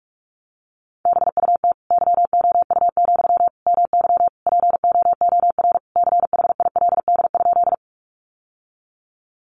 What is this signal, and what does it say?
Morse code, heard as a keyed tone, about 44 wpm, 700 Hz. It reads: BUT YOU, MY POOR CHILD?
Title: Sentences from Top 1000 Words 45wpm